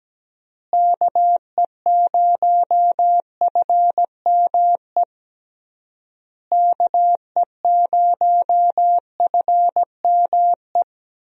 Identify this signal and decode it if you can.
Morse code, heard as a keyed tone, about 17 wpm, 700 Hz. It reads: KE0FME KE0FME
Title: USA Call Signs - 6 Characters - Repeated 3x - 17wpm